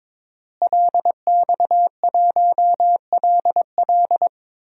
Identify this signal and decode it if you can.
Morse code, heard as a keyed tone, about 22 wpm, 700 Hz. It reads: LX1LL
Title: DXCC-100 Call Signs - 5 Characters - Repeated 2x - 22wpm